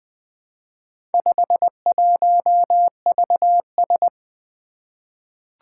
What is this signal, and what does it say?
Morse code, heard as a keyed tone, about 20 wpm, 700 Hz. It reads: 51VS
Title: Four Letter-Number Combinations - 20wpm